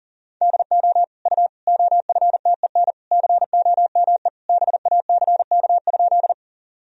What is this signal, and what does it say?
Morse code, heard as a keyed tone, about 37 wpm, 700 Hz. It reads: DO U OFTEN COME BACK?
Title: Sentences from Top 300 Words and Abbreviations 40wpm